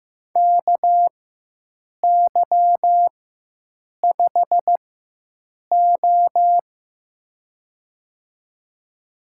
Morse code, heard as a keyed tone, about 15 wpm, spaced as for 7 wpm, 700 Hz. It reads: KY5O
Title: USA Call Signs - 4 Characters - 4x Character Spacing - 15wpm